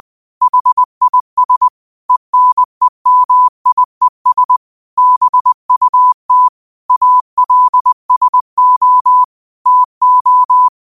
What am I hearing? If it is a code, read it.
Morse code, heard as a keyed tone, about 20 wpm, 1000 Hz. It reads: HIS ENEMIES BUT ALSO TO
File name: qotdfiles / 20wpm